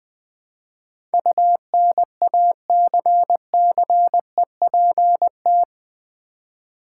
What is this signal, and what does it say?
Morse code, heard as a keyed tone, about 20 wpm, 700 Hz. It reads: UNACCEPT
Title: Build A Word - 4 Letters At A Time - 12-Letter Words - 20wpm